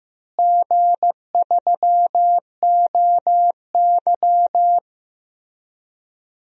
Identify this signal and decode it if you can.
Morse code, heard as a keyed tone, about 15 wpm, 700 Hz. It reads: G3OY
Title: DXCC-100 Call Signs - 4 Characters - Repeated 3x - 15wpm